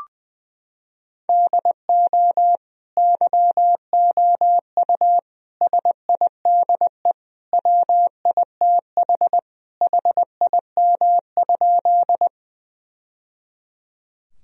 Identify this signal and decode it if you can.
Morse code, heard as a keyed tone, about 20 wpm, 700 Hz. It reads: DO YOU SIDE WITH HIM?
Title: Sentences from Top 1000 Words - Review Part 1 - 20wpm